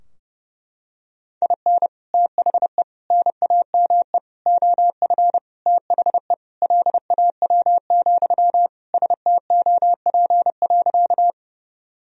Morse code, heard as a keyed tone, about 30 wpm, 700 Hz. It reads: IN THE NAME OF THE LAW, STOP.
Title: Sentences from Top 1000 Words - Review Part 3 - 30wpm